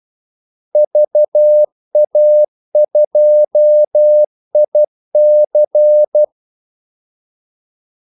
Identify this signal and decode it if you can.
Morse code, heard as a keyed tone, about 12 wpm, 600 Hz. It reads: VA2IC
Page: 2 ekstra ordmellemrum mellem kaldesignalerne